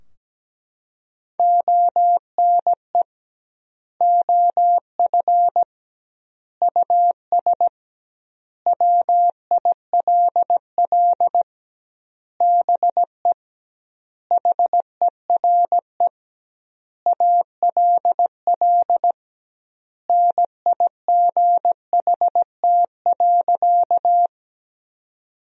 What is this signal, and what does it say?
Morse code, heard as a keyed tone, about 17 wpm, 700 Hz. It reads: ONE OF US WILL BE HERE ALL NIGHT.
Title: 2x Word-Spacing - Sentences from Top 300 Words - 17wpm